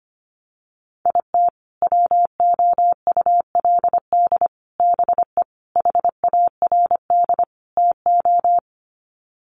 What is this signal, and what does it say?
Morse code, heard as a keyed tone, about 25 wpm, 700 Hz. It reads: IT WOULD BE HARD TO
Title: Sets of 5 Words Encore2 25wpm